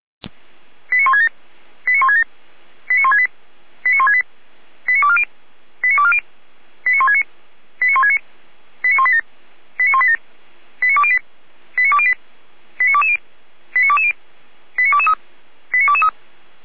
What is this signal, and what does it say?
Five-tone selective call (ZVEI), heard as quick runs of five tones, tones of about 70 ms. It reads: 89167 89167 89168 89168 89240 89240 89169 89169 89187 89187 89208 89208 89220 89220 89223 89223